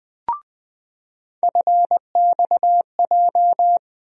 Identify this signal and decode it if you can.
Morse code, heard as a keyed tone, about 21 wpm, 700 Hz. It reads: FXJ
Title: USA Call Sign Postfixes - 3 Letters 20wpm